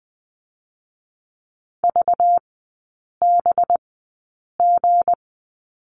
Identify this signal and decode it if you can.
Morse code, heard as a keyed tone, about 20 wpm, 700 Hz. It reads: V B G
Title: 205 Lesson 33e - Letter j - Three Characters 20wpm